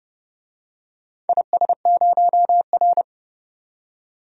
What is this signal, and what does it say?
Morse code, heard as a keyed tone, about 31 wpm, 700 Hz. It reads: IS0R
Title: DXCC-100 Call Signs - 4 Characters - 30wpm